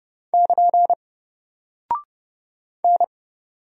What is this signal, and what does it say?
Morse code, heard as a keyed tone, about 30 wpm, 700 Hz. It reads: <KN> N
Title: Single Character (Letter:Number:Punctuation:Prosign) Alt 30wpm